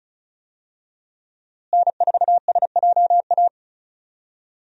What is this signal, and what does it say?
Morse code, heard as a keyed tone, about 36 wpm, 700 Hz. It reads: N4SJA
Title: USA Call Signs - 5 Characters - Repeated 2x - 35wpm